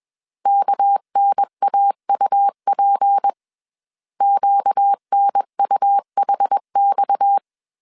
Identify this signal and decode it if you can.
Morse code, heard as a keyed tone, about 21 wpm, 800 Hz. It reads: KNAUP QNUHX
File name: CW / DET / 1 6 1500C S C 20wpm 2116